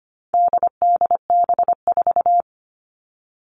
Morse code, heard as a keyed tone, about 25 wpm, 700 Hz. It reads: DDB4
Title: All Cut Numbers - 4-Digit 25wpm